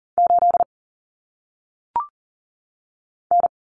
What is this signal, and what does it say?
Morse code, heard as a keyed tone, about 40 wpm, 700 Hz. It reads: 8 N
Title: Common Cut Numbers - 1-Digit 40wpm